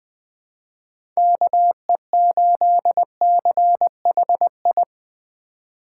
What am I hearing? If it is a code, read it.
Morse code, heard as a keyed tone, about 21 wpm, 700 Hz. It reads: KE8CHI
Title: USA Call Signs - 4x Character Spacing - 20wpm